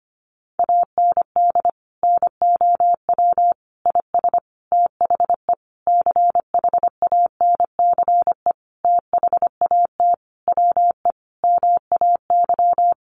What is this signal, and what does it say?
Morse code, heard as a keyed tone, about 25 wpm, 700 Hz. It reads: AND NOW IS THE CHANCE THAT WE MAY
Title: Sentences from Top 800 Words 25wpm